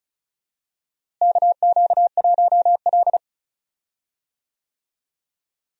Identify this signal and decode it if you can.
Morse code, heard as a keyed tone, about 34 wpm, 700 Hz. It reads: KQ1L
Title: USA Call Signs - 4 Characters - Repeated 3x - 35wpm